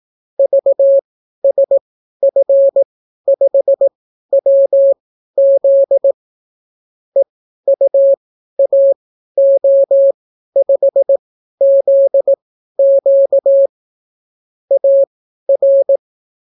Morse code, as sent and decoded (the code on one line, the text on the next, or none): ...- ... ..-. ..... .-- --.. / . ..- .- --- ..... --.. --.- / .- .-.
VSF5WZ EUAO5ZQ AR